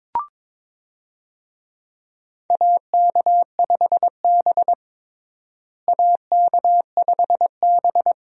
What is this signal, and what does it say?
Morse code, heard as a keyed tone, about 22 wpm, 700 Hz. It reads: AK5B AK5B
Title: USA Call Signs - 4 Characters - Repeated 2x - 22wpm